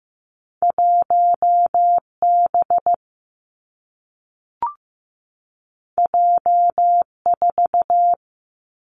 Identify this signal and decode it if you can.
Morse code, heard as a keyed tone, about 15 wpm, 700 Hz. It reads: 1 B J 4
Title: Two Challenging Letter-Number Combinations 15wpm